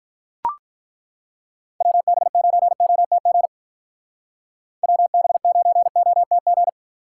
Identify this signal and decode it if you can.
Morse code, heard as a keyed tone, about 42 wpm, 700 Hz. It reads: WB9OTG WB9OTG
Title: Speed Racing - USA 2x3 Call Signs 35wpm